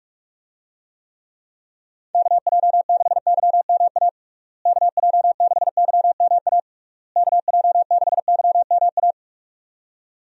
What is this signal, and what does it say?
Morse code, heard as a keyed tone, about 45 wpm, 700 Hz. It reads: KJ6YMA KJ6YMA KJ6YMA
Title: USA Call Signs - 6 Characters - Repeated 3x - 45wpm